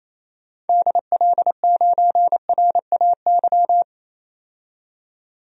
Morse code, heard as a keyed tone, about 27 wpm, 700 Hz. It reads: DL9RAY
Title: DXCC-100 Call Signs - 6 Characters - 28wpm